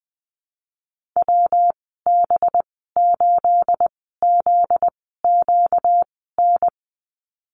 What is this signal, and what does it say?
Morse code, heard as a keyed tone, about 20 wpm, spaced as for 15 wpm, 700 Hz. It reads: WB8ZQN
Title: 245 Lesson 39g - Letter z - Call Signs 20wpm